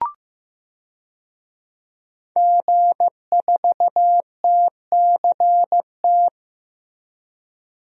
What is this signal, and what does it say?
Morse code, heard as a keyed tone, about 15 wpm, 700 Hz. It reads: G4TCT
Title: DXCC-100 Call Signs - 5 Characters - Repeated 2x - 15wpm